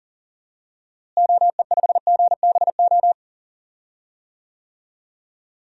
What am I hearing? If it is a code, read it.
Morse code, heard as a keyed tone, about 40 wpm, 700 Hz. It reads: OE5GBO
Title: DXCC-100 Call Signs - 6 Characters - Repeated 2x - 40wpm